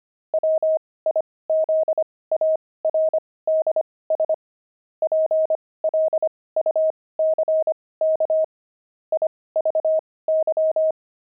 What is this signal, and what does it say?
Morse code, heard as a keyed tone, about 25 wpm, spaced as for 18 wpm, 630 Hz. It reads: WIZARDS PLUCK IVY
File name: pangram-fs-18-cs-25